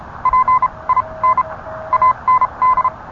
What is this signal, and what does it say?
Morse code, heard as a keyed tone, about 30 wpm, spaced as for 24 wpm, 1030 Hz. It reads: PIN AND